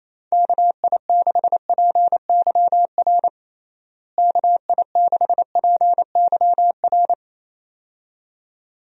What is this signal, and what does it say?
Morse code, heard as a keyed tone, about 28 wpm, 700 Hz. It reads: KI6PYR KI6PYR
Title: USA Call Signs - 6 Characters - Repeated 2x - 28wpm